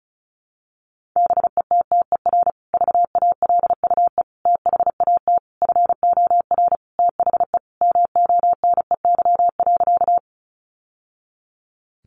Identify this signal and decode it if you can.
Morse code, heard as a keyed tone, about 34 wpm, 700 Hz. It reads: BETTER VALUE THAT FOR THE MONEY.
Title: Sentences from Top 600 Words 35wpm